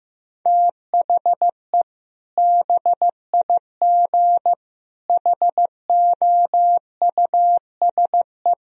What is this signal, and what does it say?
Morse code, heard as a keyed tone, about 15 wpm, 700 Hz. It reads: THE BIG HOUSE
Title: Sentences from Top 1000 Words - Review Part 1 - 15wpm